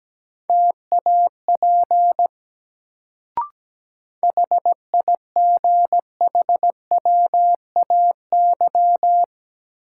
Morse code, heard as a keyed tone, about 17 wpm, 700 Hz. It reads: TAP HIGHWAY
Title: Sentences from Top 1400 Words - New Word Drill Alt - 17wpm